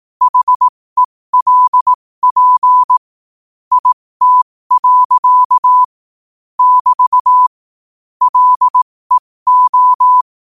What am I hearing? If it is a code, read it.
Morse code, heard as a keyed tone, about 18 wpm, spaced as for 15 wpm, 1000 Hz. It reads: HELP IT. <BT> LEO
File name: qotdfiles / 15wpm